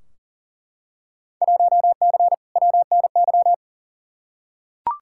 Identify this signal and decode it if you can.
Morse code, heard as a keyed tone, about 40 wpm, 700 Hz. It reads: 1C WNY
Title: Field Day Exchanges Encore - 40wpm